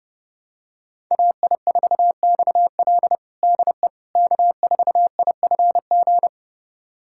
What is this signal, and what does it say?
Morse code, heard as a keyed tone, about 32 wpm, 700 Hz. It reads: AI4XL DE K4IFG
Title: USA Callsign DE Callsign - 5 Characters - 30wpm